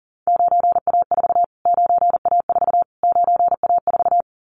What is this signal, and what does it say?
Morse code, heard as a keyed tone, about 40 wpm, 700 Hz. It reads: 9A4 9A4 9A4
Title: Callsign Prefixes for Top 25 DXCC - Repeated 3x - 40wpm